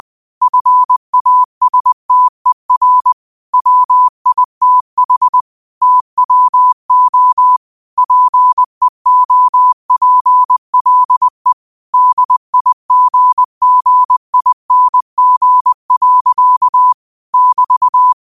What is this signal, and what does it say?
Morse code, heard as a keyed tone, about 20 wpm, 1000 Hz. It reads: FASTER WITH TWO PEOPLE DIGGING. <BT>
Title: 20 WPM morse code quotes for Thu, 29 Feb 2024